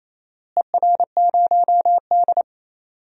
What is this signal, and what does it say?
Morse code, heard as a keyed tone, about 28 wpm, 700 Hz. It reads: ER0D
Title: DXCC-100 Call Signs - 4 Characters - Repeated 3x - 28wpm